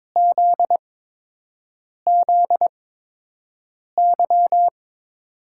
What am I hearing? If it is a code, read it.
Morse code, heard as a keyed tone, about 22 wpm, spaced as for 6 wpm, 700 Hz. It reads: ZZY
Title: USA Call Signs - 6 Characters - 8x Character Spacing - 22wpm